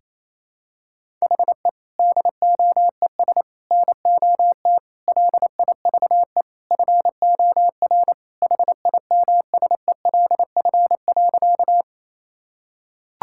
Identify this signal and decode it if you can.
Morse code, heard as a keyed tone, about 28 wpm, 700 Hz. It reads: HE DOES NOT LIVE FOR HIMSELF.
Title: Sentences from Top 1600 Words 28wpm